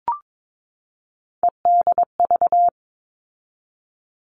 Morse code, heard as a keyed tone, about 22 wpm, 700 Hz. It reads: EDV
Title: All Cut Numbers - 3-Digit 22wpm